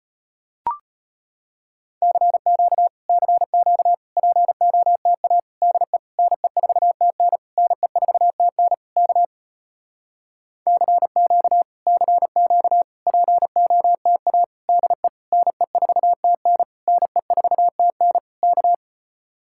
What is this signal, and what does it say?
Morse code, heard as a keyed tone, about 33 wpm, 700 Hz. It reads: CQ CQ POTA DE NE4TN NE4TN K CQ CQ POTA DE NE4TN NE4TN K